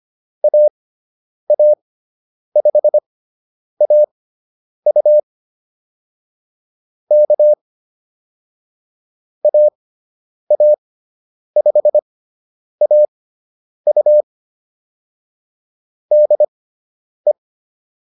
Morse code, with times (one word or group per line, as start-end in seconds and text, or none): 0.4-5.2 AA5AU
7.1-7.5 K
9.4-14.2 AA5AU
16.1-17.3 DE